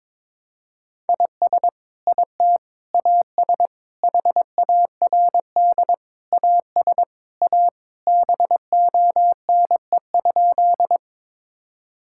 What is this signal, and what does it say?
Morse code, heard as a keyed tone, about 21 wpm, 700 Hz. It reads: IS IT AS HARD AS A BONE?
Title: Sentences from Top 1600 Words 22wpm